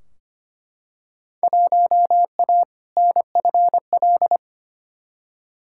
Morse code, heard as a keyed tone, about 25 wpm, 700 Hz. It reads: 1A NFL
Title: Field Day Exchanges Encore - 25wpm